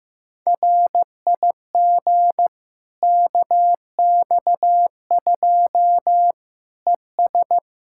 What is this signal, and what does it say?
Morse code, heard as a keyed tone, about 15 wpm, 700 Hz. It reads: RIG KX2 ES
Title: Practice QSO - Exchange 2 - Info - 15wpm